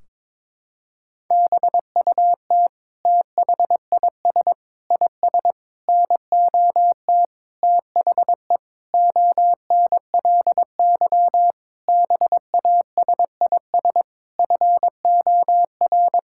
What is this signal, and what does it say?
Morse code, heard as a keyed tone, about 22 wpm, 700 Hz. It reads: BUT THIS IS NOT THE ONLY BASIS FOR
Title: Sentences from Top 1600 Words 22wpm